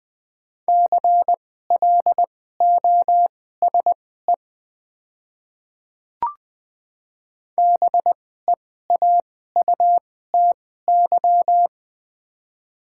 Morse code, heard as a keyed tone, about 20 wpm, spaced as for 14 wpm, 700 Hz. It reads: CLOSE BEAUTY
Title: 2x Character Spacing - Top 500 Words - 20wpm